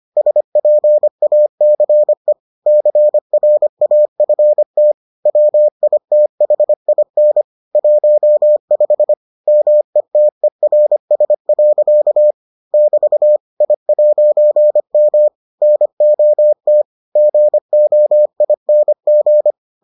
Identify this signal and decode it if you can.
Morse code, heard as a keyed tone, about 25 wpm, 600 Hz. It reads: SPACE CRAFT WITHIN 15 METERS. <BT> I'M NOT GOING